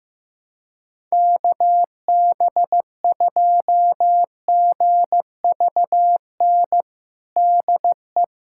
Morse code, heard as a keyed tone, about 16 wpm, 700 Hz. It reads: KB2GVN DE
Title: USA Callsign DE Callsign - 6 Characters - 15wpm